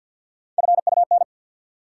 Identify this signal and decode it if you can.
Morse code, heard as a keyed tone, about 51 wpm, 700 Hz. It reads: FUN
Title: Three Letter Words - 50wpm